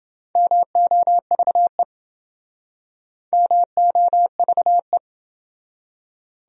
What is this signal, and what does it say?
Morse code, heard as a keyed tone, about 29 wpm, 700 Hz. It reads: MOVE MOVE